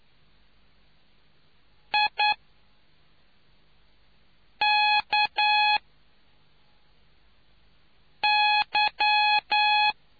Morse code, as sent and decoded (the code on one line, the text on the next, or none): .. -.- -.--
IKY